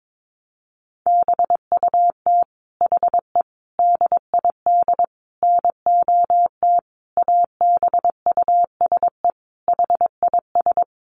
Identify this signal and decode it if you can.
Morse code, heard as a keyed tone, about 22 wpm, 700 Hz. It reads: BUT HE DID NOT ABUSE HIS
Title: Sentences from Top 1300 Words 22wpm